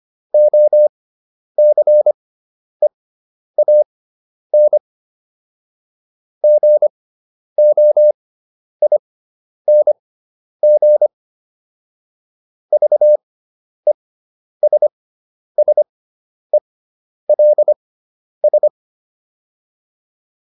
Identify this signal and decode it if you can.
Morse code, heard as a keyed tone, about 25 wpm, spaced as for 10 wpm, 600 Hz. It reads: OCEAN GOING VESSELS